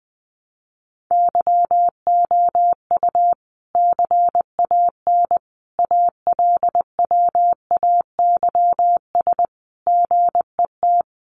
Morse code, heard as a keyed tone, about 20 wpm, 700 Hz. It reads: YOU CAN ALWAYS GET
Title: Sentences from Top 1200 Words 20wpm